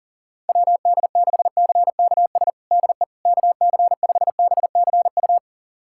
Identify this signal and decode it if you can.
Morse code, heard as a keyed tone, about 40 wpm, 700 Hz. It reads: WD6CKS DE KC5BCU
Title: USA Callsign DE Callsign - 6 Characters - 40wpm